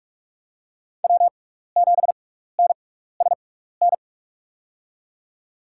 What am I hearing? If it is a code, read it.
Morse code, heard as a keyed tone, about 46 wpm, spaced as for 15 wpm, 700 Hz. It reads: W7NSN
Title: USA Call Signs - 6x Character Spacing - 45wpm